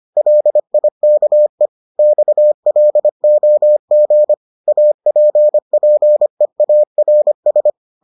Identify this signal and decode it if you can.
Morse code, heard as a keyed tone, about 25 wpm, 600 Hz. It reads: LIKE XLOG APPEARS